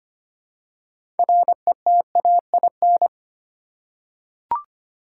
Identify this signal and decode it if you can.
Morse code, heard as a keyed tone, about 25 wpm, 700 Hz. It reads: RETAIN
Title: Six Letter Words - 25wpm